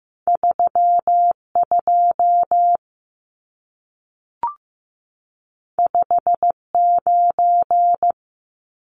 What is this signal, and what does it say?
Morse code, heard as a keyed tone, about 15 wpm, 700 Hz. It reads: 32 59
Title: Numbers - 2-Digit 15wpm